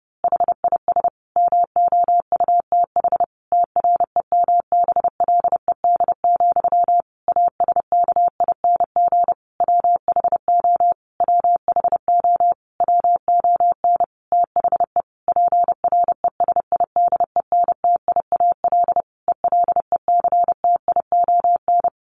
Morse code, heard as a keyed tone, about 30 wpm, 700 Hz. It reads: HIS MOUTH TREMBLED, ASKING WHO WHO WON THE PRESIDENTIAL ELECTION
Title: A Sound Of Thunder 30wpm